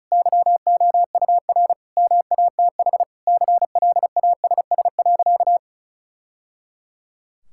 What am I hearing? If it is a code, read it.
Morse code, heard as a keyed tone, about 36 wpm, 700 Hz. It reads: YOUR MATH CLASS.
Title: Sentences from Top 1000 Words - Review Part 6 - 35wpm